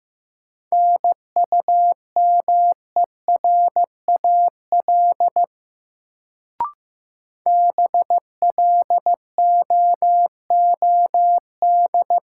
Morse code, heard as a keyed tone, about 15 wpm, 700 Hz. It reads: NUMERAL BLOOD
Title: Sentences from Top 1000 Words - Review Part 2 - New Word Drill - 15wpm